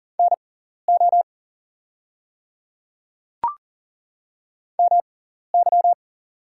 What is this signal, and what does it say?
Morse code, heard as a keyed tone, about 40 wpm, spaced as for 14 wpm, 700 Hz. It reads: NO MY
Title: 6x Character Spacing - Top 100 Words - 40wpm